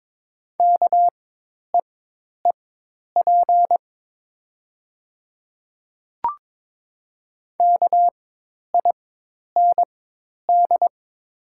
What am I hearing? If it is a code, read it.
Morse code, heard as a keyed tone, about 22 wpm, spaced as for 10 wpm, 700 Hz. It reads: KEEP KIND
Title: 4x Character Spacing - Top 500 Words - 22wpm